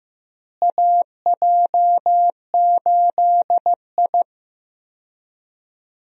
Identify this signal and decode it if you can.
Morse code, heard as a keyed tone, about 15 wpm, 700 Hz. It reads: AJ8I
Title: USA Call Signs - 4 Characters - Repeated 2x - 15wpm